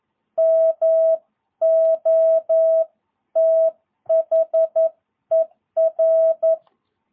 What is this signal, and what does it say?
Morse code, heard as a keyed tone, about 11 wpm, 660 Hz. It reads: MOTHER
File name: jh8cbh / 2022cw